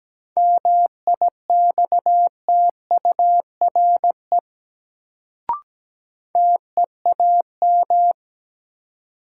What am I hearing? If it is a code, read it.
Morse code, heard as a keyed tone, about 17 wpm, 700 Hz. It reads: MIXTURE TEAM
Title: Sentences from Top 1500 Words - New Word Drill Alt - 17wpm